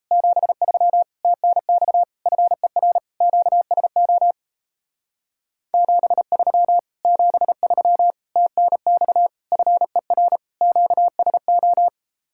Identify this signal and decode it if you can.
Morse code, heard as a keyed tone, about 37 wpm, 700 Hz. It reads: 73 TNX FER QSO 73 73 TNX FER QSO